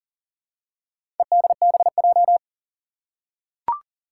Morse code, heard as a keyed tone, about 40 wpm, 700 Hz. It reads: EDBJ
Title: Four Challenging Letter-Number Combinations 40wpm